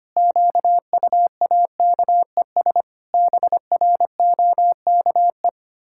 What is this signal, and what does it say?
Morse code, heard as a keyed tone, about 25 wpm, 700 Hz. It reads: QUAKES BROKE
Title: Pangram Challenge v2 - 25wpm